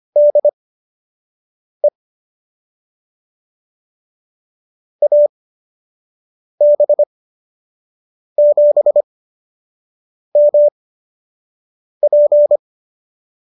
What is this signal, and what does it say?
Morse code, heard as a keyed tone, about 25 wpm, spaced as for 6 wpm, 590 Hz. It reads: DE AB7MP